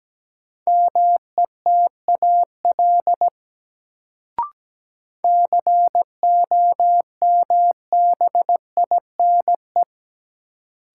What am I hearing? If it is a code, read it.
Morse code, heard as a keyed tone, about 17 wpm, 700 Hz. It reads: METAL COMBINE
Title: Sentences from Top 1600 Words - New Word Drill Alt - 17wpm